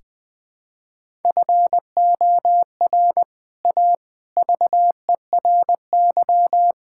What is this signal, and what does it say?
Morse code, heard as a keyed tone, about 20 wpm, 700 Hz. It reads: FOR A VERY
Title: Build A Phrase - 5-Word Phrases - 20wpm